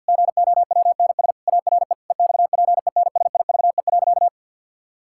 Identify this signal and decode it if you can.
Morse code, heard as a keyed tone, about 50 wpm, 700 Hz. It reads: GOWNS ARE EXPENSIVE.
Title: Pangram Challenge v2 - 50wpm